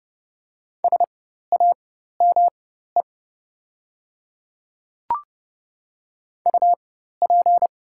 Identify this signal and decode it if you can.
Morse code, heard as a keyed tone, about 30 wpm, spaced as for 14 wpm, 690 Hz. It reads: SAME UP
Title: 4x Character Spacing - Top 200 Words - 30wpm